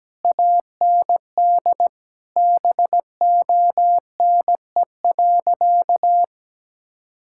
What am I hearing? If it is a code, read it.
Morse code, heard as a keyed tone, about 17 wpm, 700 Hz. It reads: AND BONE.
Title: Sentences from Top 1600 Words 17wpm